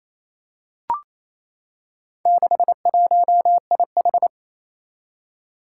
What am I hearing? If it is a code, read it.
Morse code, heard as a keyed tone, about 27 wpm, 700 Hz. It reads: 61IH
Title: Four Challenging Letter-Number Combinations 28wpm